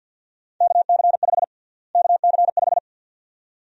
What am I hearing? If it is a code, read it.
Morse code, heard as a keyed tone, about 50 wpm, 700 Hz. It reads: KC5 KC5
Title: USA Call Sign Prefixes - Repeated 2x - 50wpm